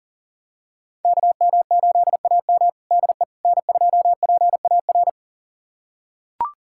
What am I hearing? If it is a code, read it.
Morse code, heard as a keyed tone, about 40 wpm, 700 Hz. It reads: KM8AM DE N2PAR
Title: USA Callsign DE Callsign - 5 Characters - 40wpm